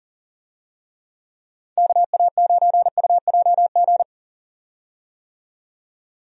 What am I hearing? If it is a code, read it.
Morse code, heard as a keyed tone, about 39 wpm, 700 Hz. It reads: KA9UJG
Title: USA Call Signs - 6 Characters - 40wpm